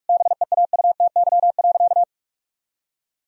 Morse code, heard as a keyed tone, about 46 wpm, 700 Hz. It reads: BEAUTY.
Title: Sentences from Top 1000 Words - Review Part 2 - 45wpm